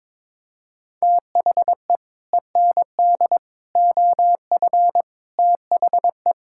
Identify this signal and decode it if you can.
Morse code, heard as a keyed tone, about 22 wpm, 700 Hz. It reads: THE END OF THE
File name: practice / 22wpm / Build A Phrase - 5-Word Phrases - 22wpm